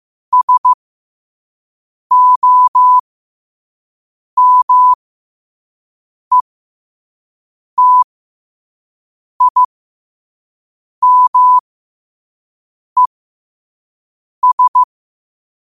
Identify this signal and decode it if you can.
Morse code, heard as a keyed tone, about 15 wpm, spaced as for 5 wpm, 1000 Hz. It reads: SOMETIMES
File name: qotdfiles / 5wpm